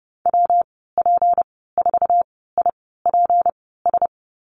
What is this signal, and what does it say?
Morse code, heard as a keyed tone, about 32 wpm, spaced as for 17 wpm, 700 Hz. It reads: WP4IPS
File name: Lesson 37g - Letter x - Call Signs 30wpm